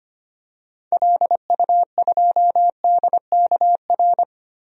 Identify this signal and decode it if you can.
Morse code, heard as a keyed tone, about 25 wpm, 700 Hz. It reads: LU2DKR
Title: DXCC-100 Call Signs - 6 Characters - Repeated 2x - 25wpm